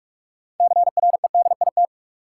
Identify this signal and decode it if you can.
Morse code, heard as a keyed tone, about 47 wpm, 700 Hz. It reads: CREDIT
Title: Sentences from Top 1000 Words - New Word Drill - 45wpm